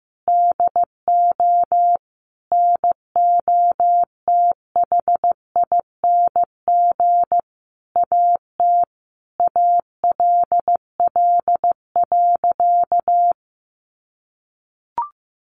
Morse code, heard as a keyed tone, about 15 wpm, 700 Hz. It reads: DO NOTHING AT ALL.
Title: Sentences from Top 1300 Words 15wpm